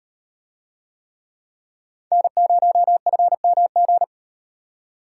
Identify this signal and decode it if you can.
Morse code, heard as a keyed tone, about 36 wpm, 700 Hz. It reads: N0FMG